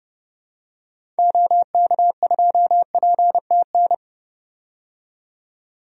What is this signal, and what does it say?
Morse code, heard as a keyed tone, about 30 wpm, 700 Hz. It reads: OK2PTN
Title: DXCC-100 Call Signs - 6 Characters - 30wpm